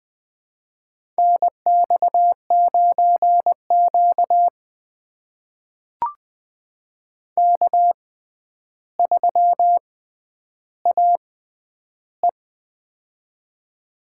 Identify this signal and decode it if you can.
Morse code, heard as a keyed tone, about 20 wpm, spaced as for 10 wpm, 700 Hz. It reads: NX9Q K3AE
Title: USA Call Signs - 4 Characters - 6x Character Spacing - 20wpm